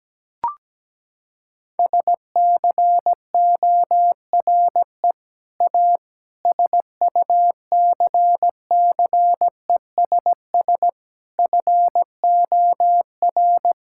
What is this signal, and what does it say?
Morse code, heard as a keyed tone, about 17 wpm, 700 Hz. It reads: SCORE A SUCCESS FOR
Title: Sentences from Top 1600 Words 17wpm